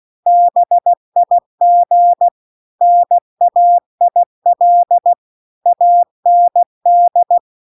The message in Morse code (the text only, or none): BIG NAIL AND